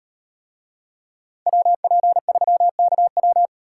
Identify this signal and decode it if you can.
Morse code, heard as a keyed tone, about 38 wpm, 700 Hz. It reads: WP3KW